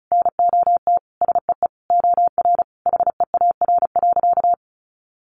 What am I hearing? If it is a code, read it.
Morse code, heard as a keyed tone, about 34 wpm, 700 Hz. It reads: NOT SEE OR HEAR.
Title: Sentences from Top 600 Words 35wpm